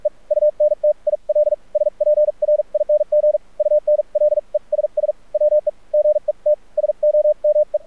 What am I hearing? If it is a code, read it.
Morse code, heard as a keyed tone, about 46 wpm, spaced as for 38 wpm, 600 Hz, with radio noise. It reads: E UNTIL SPRING UNLESS WE GET SOME
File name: archived / CRSnet